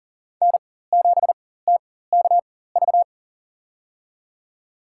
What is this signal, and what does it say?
Morse code, heard as a keyed tone, about 40 wpm, spaced as for 19 wpm, 700 Hz. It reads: N7TKV
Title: USA Call Signs - 5 Characters - 4x Character Spacing - 40wpm